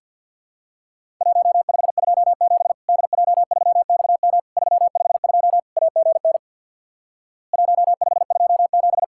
Morse code, heard as a keyed tone, about 50 wpm, 700 Hz. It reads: 1527 DJ3XM 352 AGN 1527
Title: Worked All Europe DX Contest - QTC Exchange with Repeats - 50wpm